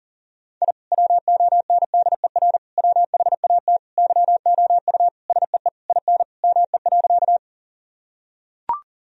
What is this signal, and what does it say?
Morse code, heard as a keyed tone, about 40 wpm, 700 Hz. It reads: I WONDER WHAT YOU SEE IN ME.
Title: Sentences from Top 1000 Words - Review Part 3 - 40wpm